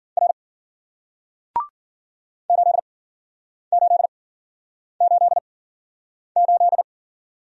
Morse code, heard as a keyed tone, about 54 wpm, spaced as for 9 wpm, 700 Hz. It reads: R 8888